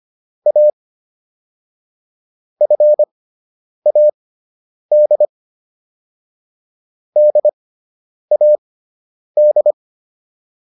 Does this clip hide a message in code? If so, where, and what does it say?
Yes: Morse code from 0.5 to 9.7 s, heard as a keyed tone, about 25 wpm, spaced as for 9 wpm, 600 Hz. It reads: A FAD DAD